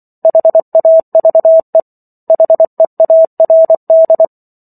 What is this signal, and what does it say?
Morse code, heard as a keyed tone, about 24 wpm, 650 Hz. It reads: HAVE HEARD